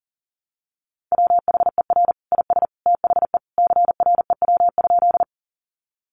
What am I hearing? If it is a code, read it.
Morse code, heard as a keyed tone, about 40 wpm, 700 Hz. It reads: WHER IS THE CREW?
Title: Sentences from Top 1500 Words 40wpm